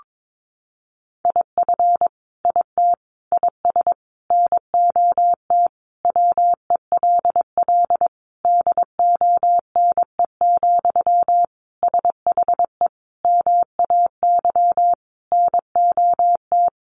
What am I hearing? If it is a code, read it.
Morse code, heard as a keyed tone, about 22 wpm, 700 Hz. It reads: IF IT IS NOT WELL DONE, SHE MAY NOT
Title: Sentences from Top 1200 Words 22wpm